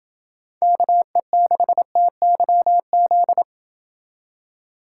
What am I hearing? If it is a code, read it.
Morse code, heard as a keyed tone, about 28 wpm, 700 Hz. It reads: KE6TYZ